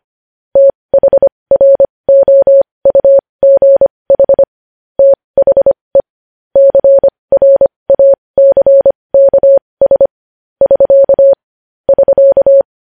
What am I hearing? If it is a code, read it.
Morse code, heard as a keyed tone, about 25 wpm, spaced as for 20 wpm, 550 Hz. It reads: THROUGH THE CRACKS <SK> <SK>
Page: news-headlines-in-morse-code-at-20-wpm